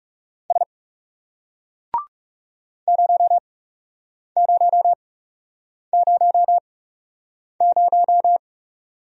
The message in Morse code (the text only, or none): S 0000